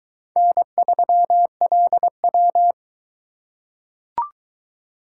Morse code, heard as a keyed tone, about 20 wpm, 700 Hz. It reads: N3LW